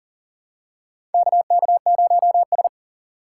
Morse code, heard as a keyed tone, about 40 wpm, 700 Hz. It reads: KK0S
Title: USA Call Signs - 4 Characters - 4x Character Spacing - 40wpm